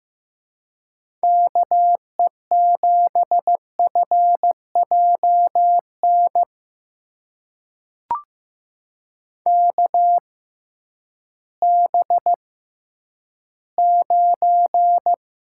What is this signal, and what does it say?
Morse code, heard as a keyed tone, about 15 wpm, spaced as for 10 wpm, 700 Hz. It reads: KE7FJN KB9
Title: USA Call Signs - 6x Character Spacing - 15wpm